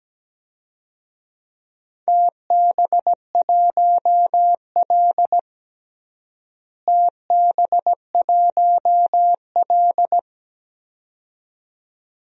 Morse code, heard as a keyed tone, about 17 wpm, 700 Hz. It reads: TB1L TB1L
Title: DXCC-100 Call Signs - 4 Characters - Repeated 2x - 17wpm